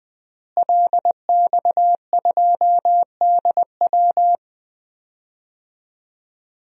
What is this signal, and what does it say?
Morse code, heard as a keyed tone, about 20 wpm, 700 Hz. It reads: LX2DW
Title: DXCC-100 Call Signs - 5 Characters - Repeated 2x - 20wpm